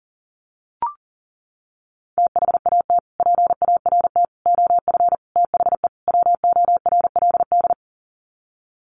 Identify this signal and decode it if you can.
Morse code, heard as a keyed tone, about 40 wpm, 700 Hz. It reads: THAT PART OF THE WORLD
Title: Sets of 5 Words Encore2 40wpm